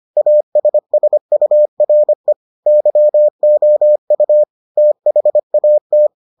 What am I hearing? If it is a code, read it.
Morse code, heard as a keyed tone, about 25 wpm, 600 Hz. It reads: ASSURE YOU THAT